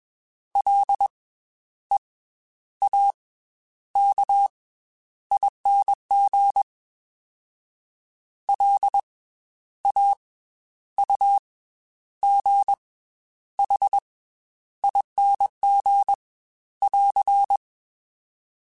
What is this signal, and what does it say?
Morse code, heard as a keyed tone, about 21 wpm, 800 Hz. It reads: L E A K ING L A U G H ING <AR>
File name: Special INGs / Special INGs-mp3s / Slow ing @20